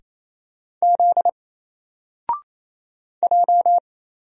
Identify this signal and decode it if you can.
Morse code, heard as a keyed tone, about 27 wpm, 700 Hz. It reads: Z J
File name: Single-Letter-Number-28wpm